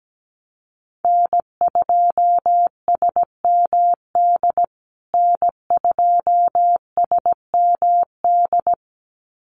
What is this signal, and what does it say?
Morse code, heard as a keyed tone, about 17 wpm, 700 Hz. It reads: N2SMD N2SMD
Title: USA Call Signs Repeated 2x 17wpm